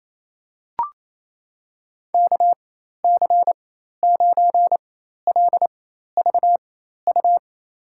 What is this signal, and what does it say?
Morse code, heard as a keyed tone, about 27 wpm, spaced as for 13 wpm, 700 Hz. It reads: KC9LVU
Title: USA Call Signs - 4x Character Spacing - 28wpm